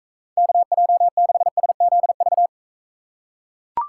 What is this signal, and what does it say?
Morse code, heard as a keyed tone, about 41 wpm, 700 Hz. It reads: KJ6SZV